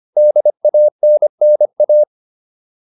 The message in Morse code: DANNA

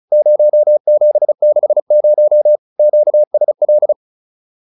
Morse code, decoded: 0760 QSL

